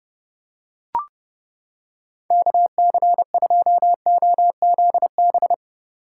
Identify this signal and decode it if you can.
Morse code, heard as a keyed tone, about 29 wpm, 700 Hz. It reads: KC2OZB